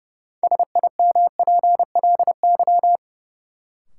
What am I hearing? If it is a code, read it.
Morse code, heard as a keyed tone, about 32 wpm, 700 Hz. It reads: SIMPLY
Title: Sentences from Top 600 Words - New Word Drill - 30wpm